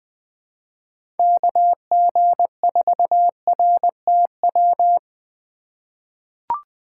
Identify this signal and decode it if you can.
Morse code, heard as a keyed tone, about 20 wpm, 700 Hz. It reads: KG4RTW
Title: USA Call Signs - 6 Characters - 3x Character Spacing - 20wpm